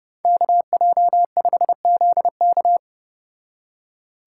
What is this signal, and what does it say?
Morse code, heard as a keyed tone, about 29 wpm, 700 Hz. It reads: KJ5ZK